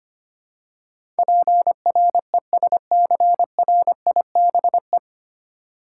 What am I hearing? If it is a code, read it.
Morse code, heard as a keyed tone, about 25 wpm, 700 Hz. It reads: PRESCRIBE
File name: practice / 25wpm / 9 Letter Words - First 3 Letters Given - 25wpm